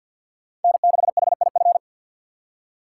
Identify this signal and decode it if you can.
Morse code, heard as a keyed tone, about 53 wpm, 690 Hz. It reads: N6HIF
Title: USA Call Signs - 5 Characters - 2x Character Spacing - 50wpm